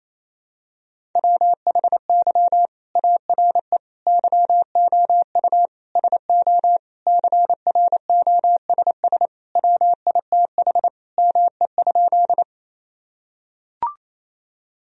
Morse code, heard as a keyed tone, about 28 wpm, 700 Hz. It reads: WHY ARE YOU SO CROSS WITH ME?
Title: Sentences from Top 1000 Words - Review Part 1 - 28wpm